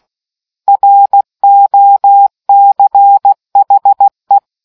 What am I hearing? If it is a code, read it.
Morse code, heard as a keyed tone, about 16 wpm, 800 Hz. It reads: ROCHE